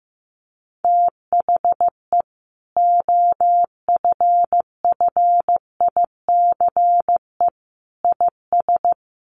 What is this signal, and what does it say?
Morse code, heard as a keyed tone, about 15 wpm, 700 Hz. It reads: THE OFFICE IS
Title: Sentences from Top 1000 Words 15wpm